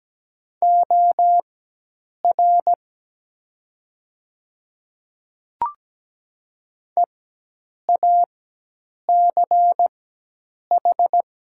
Morse code, heard as a keyed tone, about 17 wpm, spaced as for 8 wpm, 700 Hz. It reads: OR EACH
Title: 4x Character Spacing - Top 100 Words - 17wpm